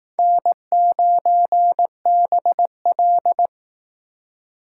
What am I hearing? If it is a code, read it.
Morse code, heard as a keyed tone, about 19 wpm, 700 Hz. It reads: N9BL
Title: Speed Racing - USA 1x2 Call Signs 15wpm